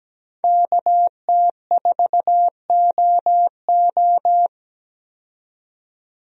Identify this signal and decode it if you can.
Morse code, heard as a keyed tone, about 17 wpm, 700 Hz. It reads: KT4OO
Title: USA Call Signs - 5 Characters - 17wpm